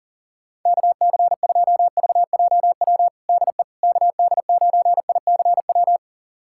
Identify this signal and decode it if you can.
Morse code, heard as a keyed tone, about 40 wpm, 700 Hz. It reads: KC2VJW DE KD9ICW
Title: USA Callsign DE Callsign 40wpm